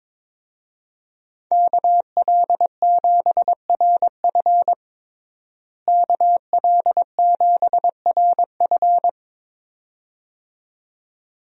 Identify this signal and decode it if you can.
Morse code, heard as a keyed tone, about 22 wpm, 700 Hz. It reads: KL7RF KL7RF
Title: DXCC-100 Call Signs - 5 Characters - Repeated 2x - 22wpm